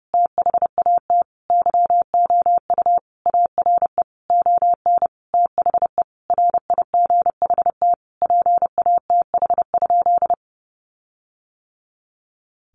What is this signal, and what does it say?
Morse code, heard as a keyed tone, about 30 wpm, 700 Hz. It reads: THAT YOU ARE ON THE RIGHT PATH?
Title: Sentences from Top 1400 Words 30wpm